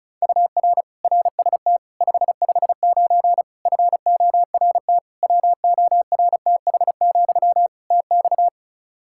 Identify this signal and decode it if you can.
Morse code, heard as a keyed tone, about 36 wpm, 700 Hz. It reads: UR RST 559 FORT WORTH, TX
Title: Practice QSO - Exchange 1 - Beginning - 35wpm